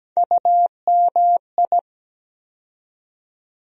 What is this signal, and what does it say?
Morse code, heard as a keyed tone, about 17 wpm, 700 Hz. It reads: UMI
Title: USA Call Sign Postfixes - 3 Letters 17wpm